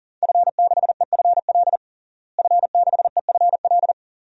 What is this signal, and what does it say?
Morse code, heard as a keyed tone, about 40 wpm, 700 Hz. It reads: F6EFL F6EFL
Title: DXCC-100 Call Signs - 5 Characters - Repeated 2x - 40wpm